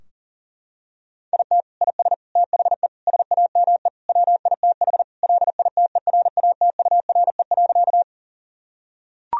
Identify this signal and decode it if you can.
Morse code, heard as a keyed tone, about 40 wpm, 700 Hz. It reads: IT IS THE SAME WITH LITERATURE.
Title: Sentences from Top 1000 Words - Review Part 3 - 40wpm